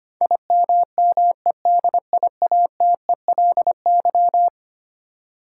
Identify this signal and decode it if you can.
Morse code, heard as a keyed tone, about 25 wpm, 700 Hz. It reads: IMMEDIATELY
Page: morse-code-ninja-practice-25wpm